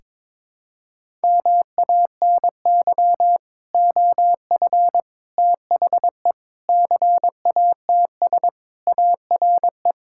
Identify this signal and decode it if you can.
Morse code, heard as a keyed tone, about 22 wpm, 700 Hz. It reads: MANY OF THE CATS ARE
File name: practice / 22wpm / Sentences with 3-Word Phrases - 22wpm